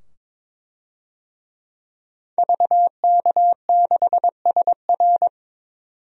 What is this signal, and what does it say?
Morse code, heard as a keyed tone, about 22 wpm, 700 Hz. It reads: VK6SR
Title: DXCC-100 Call Signs - 5 Characters - 22wpm